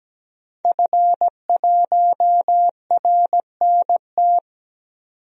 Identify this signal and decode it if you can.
Morse code, heard as a keyed tone, about 17 wpm, 700 Hz. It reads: F1RNT
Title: DXCC-100 Call Signs - 5 Characters - 17wpm